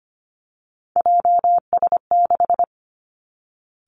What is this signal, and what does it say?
Morse code, heard as a keyed tone, about 25 wpm, 700 Hz. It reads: JS6
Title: Three Challenging Letter-Number Combinations 25wpm